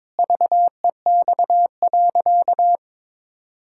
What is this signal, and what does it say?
Morse code, heard as a keyed tone, about 21 wpm, 700 Hz. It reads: VEX.
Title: Pangram Challenge v2 - 22wpm